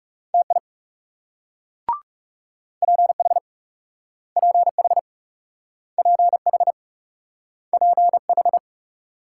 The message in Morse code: TI PH PH PH PH

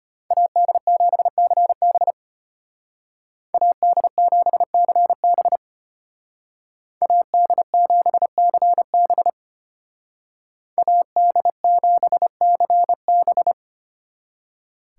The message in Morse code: AD7CB AD7CB AD7CB AD7CB

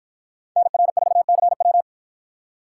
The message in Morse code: NR4CW

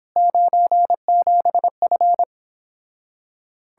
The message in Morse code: 97F

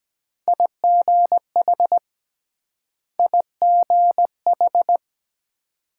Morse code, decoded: IGH IGH

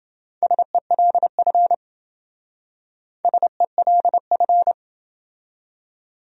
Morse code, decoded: SELF SELF

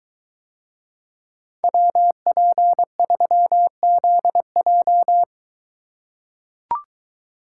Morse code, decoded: WP3ZJ

